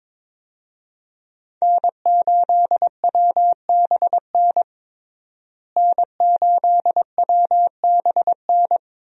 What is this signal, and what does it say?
Morse code, heard as a keyed tone, about 22 wpm, 700 Hz. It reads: N8WBN N8WBN
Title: USA Call Signs - 5 Characters - Repeated 2x - 22wpm